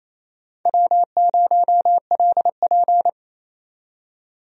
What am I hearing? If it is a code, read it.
Morse code, heard as a keyed tone, about 28 wpm, 700 Hz. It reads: W0LP
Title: USA Call Signs - 4 Characters - 28wpm